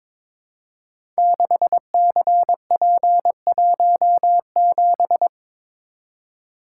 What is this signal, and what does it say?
Morse code, heard as a keyed tone, about 22 wpm, 700 Hz. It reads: 6CP17
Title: Code Groups - 5 Letter:Numbers - 22wpm